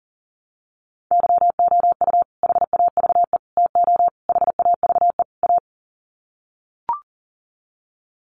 Morse code, heard as a keyed tone, about 37 wpm, 700 Hz. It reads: YOU HAVE TO HAVE A
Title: Sets of 5 Words Encore 40wpm